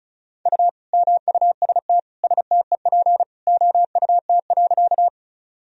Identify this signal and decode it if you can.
Morse code, heard as a keyed tone, about 36 wpm, 700 Hz. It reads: U MUST STEP OUT.
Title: Sentences from Top 400 Words and Abbreviations 35wpm